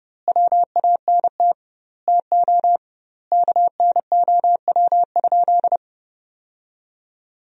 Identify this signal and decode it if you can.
Morse code, heard as a keyed tone, about 30 wpm, 700 Hz. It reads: WANT TO KNOW?
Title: 2x Word-Spacing - Sentences from Top 200 Words - 30wpm